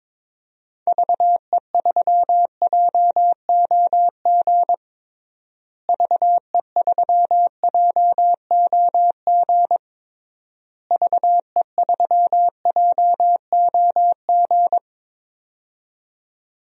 Morse code, heard as a keyed tone, about 21 wpm, 700 Hz. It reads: VE3JOG VE3JOG VE3JOG
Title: DXCC-100 Call Signs - 6 Characters - Repeated 3x - 22wpm